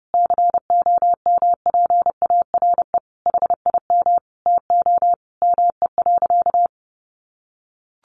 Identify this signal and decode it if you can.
Morse code, heard as a keyed tone, about 30 wpm, 700 Hz. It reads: COMPARE HIM TO ME.
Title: Sentences from Top 1300 Words 30wpm